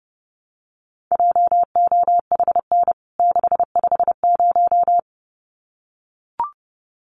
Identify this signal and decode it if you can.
Morse code, heard as a keyed tone, about 30 wpm, 700 Hz. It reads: JOHN 650
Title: CWT Exchange 30wpm